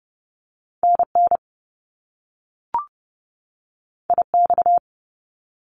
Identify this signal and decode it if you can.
Morse code, heard as a keyed tone, about 30 wpm, 700 Hz. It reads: NN IX
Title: Top Two Letter Combinations in Words 30wpm